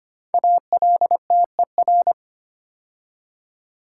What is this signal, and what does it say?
Morse code, heard as a keyed tone, about 25 wpm, 700 Hz. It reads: ALTER
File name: 5 Letter Words - First 3 Letters Given - 25wpm